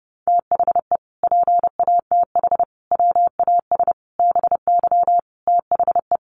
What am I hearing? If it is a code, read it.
Morse code, heard as a keyed tone, about 30 wpm, 700 Hz. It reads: THE PATH WAS BY THE
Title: Sentences from Top 1400 Words 30wpm